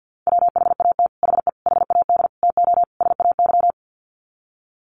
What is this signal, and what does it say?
Morse code, heard as a keyed tone, about 50 wpm, 700 Hz. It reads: WHAT HE HAD TO SAY
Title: Sets of 5 Words Encore2 50wpm